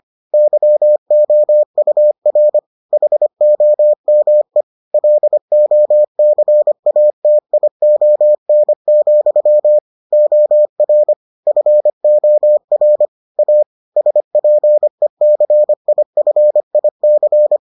Morse code, heard as a keyed tone, about 25 wpm, 600 Hz. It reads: YOUR HOME LOCATION, OR FOR A SPECIFIC